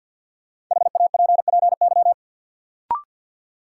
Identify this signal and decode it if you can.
Morse code, heard as a keyed tone, about 46 wpm, 700 Hz. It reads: HAPPY
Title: Sentences from Top 1000 Words - New Word Drill - 50wpm